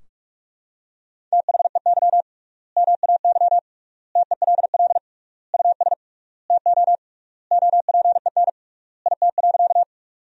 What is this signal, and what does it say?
Morse code, heard as a keyed tone, about 41 wpm, 700 Hz. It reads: THEY MAY TELL US TO OPEN IT.
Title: 3x Word-Spacing - Sentences from Top 300 Words - 45wpm